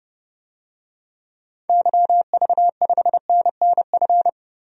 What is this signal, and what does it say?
Morse code, heard as a keyed tone, about 31 wpm, 700 Hz. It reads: YV5NNF
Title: DXCC-100 Call Signs - 6 Characters - Repeated 3x - 30wpm